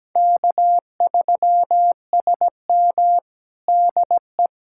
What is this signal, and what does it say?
Morse code, heard as a keyed tone, about 17 wpm, 700 Hz. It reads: K3SM DE
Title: USA Callsign DE Callsign - 4 Characters - 17wpm